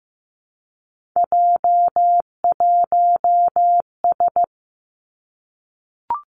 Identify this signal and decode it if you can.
Morse code, heard as a keyed tone, about 15 wpm, 700 Hz. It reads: J1S
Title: Three Challenging Letter-Number Combinations 15wpm